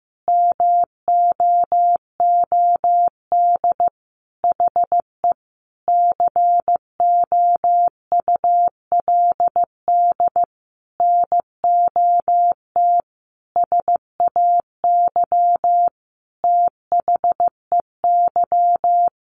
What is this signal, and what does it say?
Morse code, heard as a keyed tone, about 15 wpm, 700 Hz. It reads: MOOD HE COULD NOT SAY THEY
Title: Sentences from Top 1100 Words 15wpm